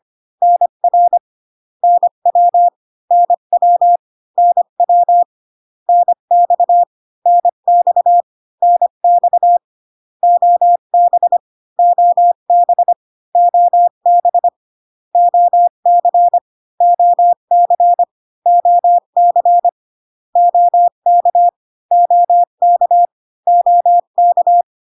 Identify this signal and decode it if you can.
Morse code, heard as a keyed tone, about 25 wpm, 700 Hz. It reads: NR NW NW NW NX NX NX OB OB OB OC OC OC OK OK OK